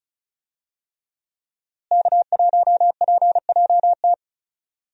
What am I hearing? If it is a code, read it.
Morse code, heard as a keyed tone, about 36 wpm, 700 Hz. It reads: K1PJT
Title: USA Call Signs - 5 Characters - Repeated 3x - 35wpm